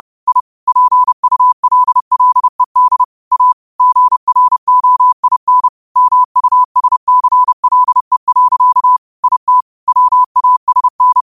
Morse code, heard as a keyed tone, about 30 wpm, 1000 Hz. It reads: I PULLED A GROIN MUSCLE. IT WASN